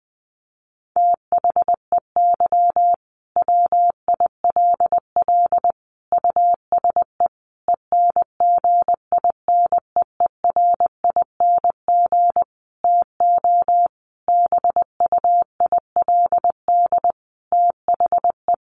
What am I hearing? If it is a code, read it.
Morse code, heard as a keyed tone, about 20 wpm, 700 Hz. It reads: THEY WILL USE ENGINEERING TO BUILD THE
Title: Sentences from Top 1500 Words 20wpm